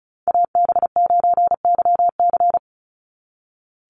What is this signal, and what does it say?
Morse code, heard as a keyed tone, about 36 wpm, 700 Hz. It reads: AB9YC
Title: Call Signs - 35wpm